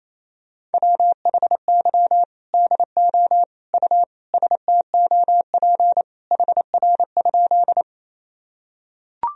Morse code, heard as a keyed tone, about 28 wpm, 700 Hz. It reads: WHY DO U STOP HR?
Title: Sentences from Top 300 Words and Abbreviations 28wpm